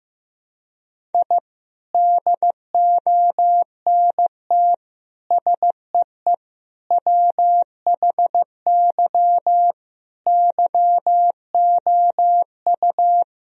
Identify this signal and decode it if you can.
Morse code, heard as a keyed tone, about 15 wpm, 700 Hz. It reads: I DONT SEE WHY YOU
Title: Sentences from Top 1000 Words - Review Part 2 - 15wpm